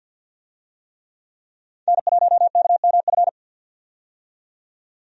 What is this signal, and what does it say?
Morse code, heard as a keyed tone, about 48 wpm, 700 Hz. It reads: N1KMF
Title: USA Call Signs - 5 Characters - 50wpm